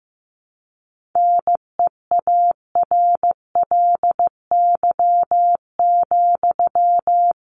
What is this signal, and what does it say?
Morse code, heard as a keyed tone, about 15 wpm, 700 Hz. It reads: NEARLY,
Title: Sentences from Top 1300 Words 15wpm